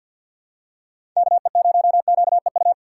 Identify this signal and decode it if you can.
Morse code, heard as a keyed tone, about 48 wpm, 700 Hz. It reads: KE0QEU
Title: USA Call Signs - 6 Characters - 6x Character Spacing - 50wpm